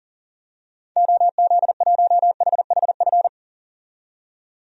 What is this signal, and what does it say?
Morse code, heard as a keyed tone, about 39 wpm, 700 Hz. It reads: OZ1HHF
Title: DXCC-100 Call Signs - 6 Characters - 40wpm